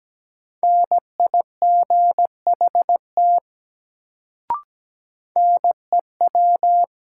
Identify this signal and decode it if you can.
Morse code, heard as a keyed tone, about 17 wpm, 700 Hz. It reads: NIGHT NEW
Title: Sentences from Top 300 Words - New Word Drill - 17wpm